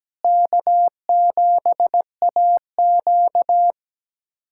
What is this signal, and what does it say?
Morse code, heard as a keyed tone, about 17 wpm, 700 Hz. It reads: K7AQ
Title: USA Callsign DE Callsign - 4 Characters - 17wpm